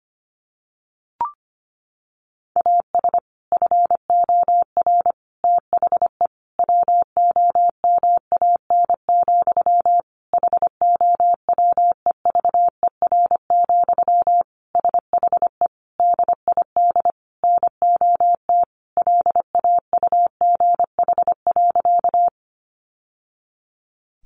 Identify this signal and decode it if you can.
Morse code, heard as a keyed tone, about 25 wpm, 700 Hz. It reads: AS FOR THE WOMAN, HOWEVER, SHE DID NOT LAUGH.
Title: Sentences from Top 1100 Words 25wpm